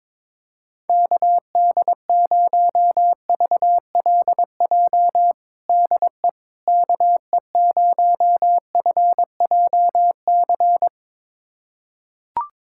Morse code, heard as a keyed tone, about 22 wpm, 700 Hz. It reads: KD0VLJ DE KE0FJC
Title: USA Callsign DE Callsign - 6 Characters - 22wpm